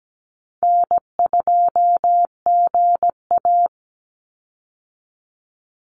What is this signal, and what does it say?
Morse code, heard as a keyed tone, about 17 wpm, 700 Hz. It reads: N2GA
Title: Easy Call Signs Repeated 2x - 17wpm